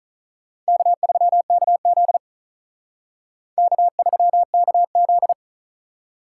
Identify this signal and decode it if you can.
Morse code, heard as a keyed tone, about 45 wpm, 700 Hz. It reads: K3KZ K3KZ